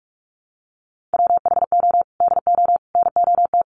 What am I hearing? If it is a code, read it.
Morse code, heard as a keyed tone, about 44 wpm, 690 Hz. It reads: WHO DO NOT
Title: Sets of 5 Words Encore2 45wpm